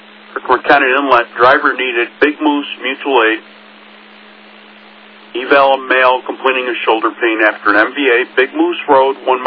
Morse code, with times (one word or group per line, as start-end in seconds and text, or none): none